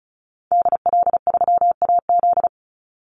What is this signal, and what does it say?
Morse code, heard as a keyed tone, about 34 wpm, 700 Hz. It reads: DL3AZ
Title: Call Signs - 35wpm